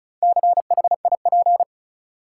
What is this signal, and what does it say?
Morse code, heard as a keyed tone, about 34 wpm, 700 Hz. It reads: CHIP
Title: Four Letter Words - 35wpm